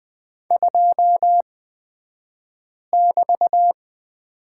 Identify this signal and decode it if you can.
Morse code, heard as a keyed tone, about 20 wpm, spaced as for 5 wpm, 700 Hz. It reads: 2<BT>
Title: 223 Lesson 36d - BT - Two Characters 20wpm